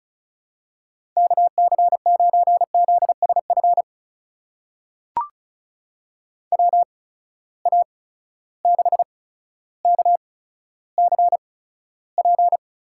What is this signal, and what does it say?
Morse code, heard as a keyed tone, about 36 wpm, spaced as for 15 wpm, 700 Hz. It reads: KC9ZSF WA6KCP
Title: USA Call Signs - 6 Characters - 8x Character Spacing - 35wpm